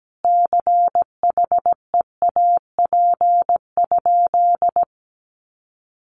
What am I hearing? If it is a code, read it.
Morse code, heard as a keyed tone, about 17 wpm, 700 Hz. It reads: CHEAP?
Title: Sentences from Top 1200 Words 17wpm